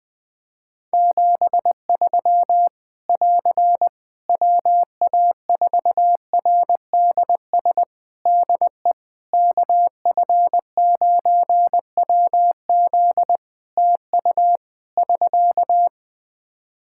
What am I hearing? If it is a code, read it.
Morse code, heard as a keyed tone, about 20 wpm, 700 Hz. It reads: 73 <AR> WA4RDS DE KF9WZ TU <SK>
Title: K1USN SST QSO Practice 20wpm